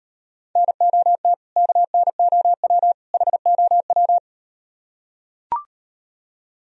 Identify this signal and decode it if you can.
Morse code, heard as a keyed tone, about 37 wpm, 700 Hz. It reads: NOT KNOW HOW